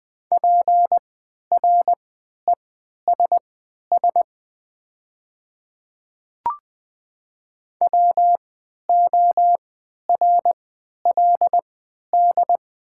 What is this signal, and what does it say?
Morse code, heard as a keyed tone, about 20 wpm, spaced as for 11 wpm, 690 Hz. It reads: PRESS WORLD
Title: 3x Character Spacing - Top 500 Words - 20wpm